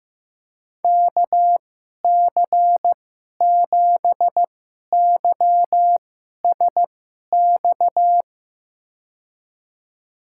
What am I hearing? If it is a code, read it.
Morse code, heard as a keyed tone, about 15 wpm, spaced as for 11 wpm, 700 Hz. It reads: KC7YSX
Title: USA Call Signs - 2x Character Spacing - 15wpm